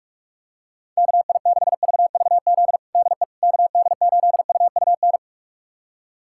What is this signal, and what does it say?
Morse code, heard as a keyed tone, about 47 wpm, 700 Hz. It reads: KI6VVZ DE KD8UUN
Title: USA Callsign DE Callsign - 6 Characters - 45wpm